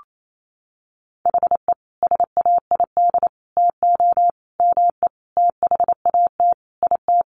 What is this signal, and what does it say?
Morse code, heard as a keyed tone, about 28 wpm, 700 Hz. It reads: HE SAID TO ME THAT IT
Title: Sentences from Top 1500 Words 28wpm